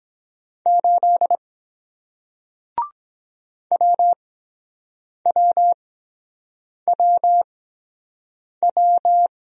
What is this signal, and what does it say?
Morse code, heard as a keyed tone, about 23 wpm, spaced as for 7 wpm, 700 Hz. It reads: 8 WWWW